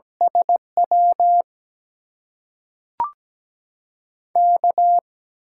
Morse code, heard as a keyed tone, about 17 wpm, 700 Hz. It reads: SW K
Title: USA Call Signs - 6 Characters - 8x Character Spacing - 17wpm